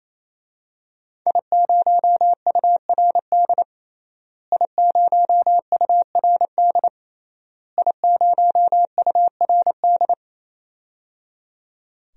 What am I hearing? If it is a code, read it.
Morse code, heard as a keyed tone, about 28 wpm, 700 Hz. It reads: I0URD I0URD I0URD
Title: DXCC-100 Call Signs - 5 Characters - Repeated 3x - 28wpm